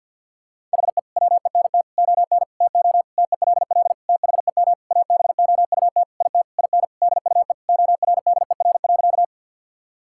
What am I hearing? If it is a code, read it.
Morse code, heard as a keyed tone, about 51 wpm, 700 Hz. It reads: HE WENT ON TO TELL THEM ABOUT IT IN DUE ORDER.
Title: Sentences from Top 1000 Words - Review Part 6 - 50wpm